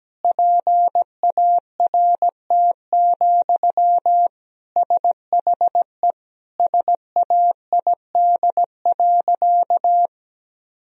Morse code, heard as a keyed tone, about 17 wpm, 700 Hz. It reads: PART, SHE SAID.
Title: Sentences from Top 1000 Words - Review Part 1 - 17wpm